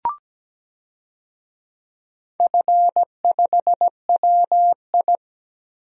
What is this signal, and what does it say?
Morse code, heard as a keyed tone, about 17 wpm, 700 Hz. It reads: F5WI
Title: DXCC-100 Call Signs - 4 Characters - Repeated 3x - 17wpm